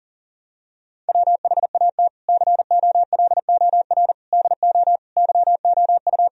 Morse code, heard as a keyed tone, about 40 wpm, 700 Hz. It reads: WHAT COLOR DO YOU
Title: Sentences from Top 1000 Words - Review Part 2 - 40wpm